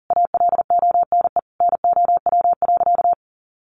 Morse code, heard as a keyed tone, about 40 wpm, 700 Hz. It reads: ALONE NOW.
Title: Sentences from Top 1100 Words 40wpm